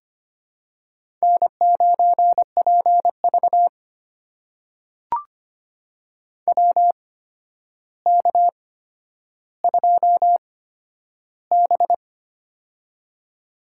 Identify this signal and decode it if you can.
Morse code, heard as a keyed tone, about 25 wpm, spaced as for 11 wpm, 700 Hz. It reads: N9PV WK2B
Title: USA Call Signs - 4 Characters - 8x Character Spacing - 25wpm